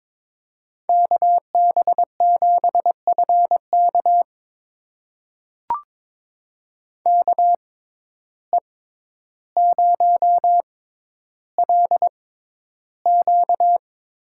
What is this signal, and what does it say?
Morse code, heard as a keyed tone, about 21 wpm, spaced as for 11 wpm, 700 Hz. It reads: KB7FK KE0LQ
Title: USA Call Signs - 5 Characters - 6x Character Spacing - 22wpm